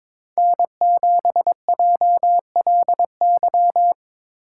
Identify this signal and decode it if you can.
Morse code, heard as a keyed tone, about 22 wpm, 700 Hz. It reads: N7JLY
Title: USA Call Signs - 8x Character Spacing - 22wpm